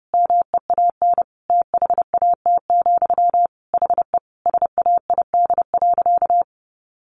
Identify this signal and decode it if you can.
Morse code, heard as a keyed tone, about 30 wpm, 700 Hz. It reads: MEAN THAT, HE SAID.
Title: Sentences from Top 600 Words 30wpm